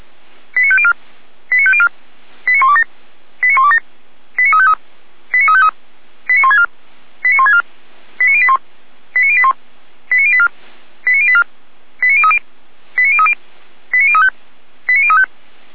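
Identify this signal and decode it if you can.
Five-tone selective call (ZVEI), heard as quick runs of five tones, tones of about 70 ms. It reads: 89504 89504 89127 89127 89353 89353 89164 89164 89081 89081 89084 89084 89030 89030 89035 89035
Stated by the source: ZVEI five-tone selective call received via SDR, from the Feuerwehr Orlamünde audio server.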